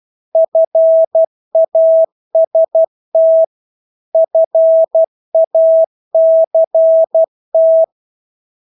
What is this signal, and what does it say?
Morse code, heard as a keyed tone, about 12 wpm, 650 Hz. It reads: FAST FACT
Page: Engelske ord 12 wpm